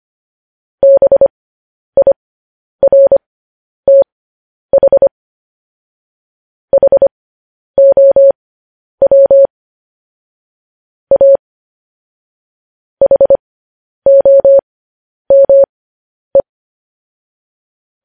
Morse code, heard as a keyed tone, about 25 wpm, spaced as for 10 wpm, 550 Hz. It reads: BIRTH HOW A HOME
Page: news headlines in morse code at 10 wpm